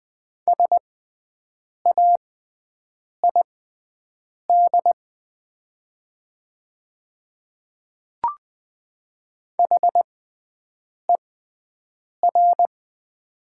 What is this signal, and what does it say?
Morse code, heard as a keyed tone, about 20 wpm, spaced as for 7 wpm, 700 Hz. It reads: SAID HER
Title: 6x Character Spacing - Top 300 Words - 20wpm